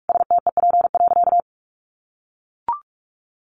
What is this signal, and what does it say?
Morse code, heard as a keyed tone, about 44 wpm, 700 Hz. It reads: STEP.
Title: Sentences from Top 1400 Words 45wpm